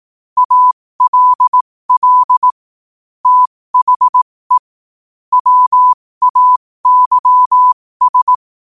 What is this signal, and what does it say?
Morse code, heard as a keyed tone, about 18 wpm, spaced as for 15 wpm, 1000 Hz. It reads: ALL THE WAYS
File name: qotdfiles / 15wpm